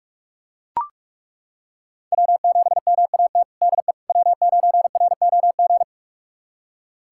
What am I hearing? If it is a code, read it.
Morse code, heard as a keyed tone, about 43 wpm, 700 Hz. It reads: W7MAT DE W9ROG
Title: USA Callsign DE Callsign - 5 Characters - 45wpm